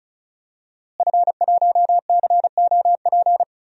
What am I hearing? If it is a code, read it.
Morse code, heard as a keyed tone, about 36 wpm, 700 Hz. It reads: F1COP
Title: DXCC-100 Call Signs - 5 Characters - Repeated 3x - 35wpm